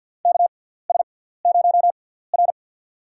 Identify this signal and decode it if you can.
Morse code, heard as a keyed tone, about 47 wpm, spaced as for 17 wpm, 700 Hz. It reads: KS0R
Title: USA Call Signs - 6x Character Spacing - 50wpm